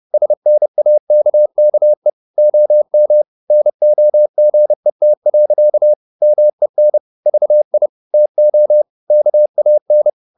Morse code, heard as a keyed tone, about 30 wpm, 600 Hz. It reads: SNAKKE OM NOGET. MEN VI TO KAN
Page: Ferie 30 wpm